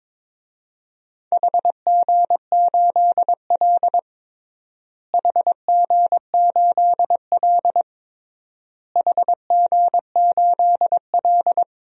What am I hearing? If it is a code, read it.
Morse code, heard as a keyed tone, about 21 wpm, 700 Hz. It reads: HG8L HG8L HG8L
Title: DXCC-100 Call Signs - 4 Characters - Repeated 3x - 22wpm